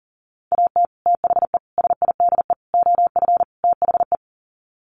Morse code, heard as a keyed tone, about 40 wpm, 700 Hz. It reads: AT THE SIDE OF THE
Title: Sets of 5 Words Encore2 40wpm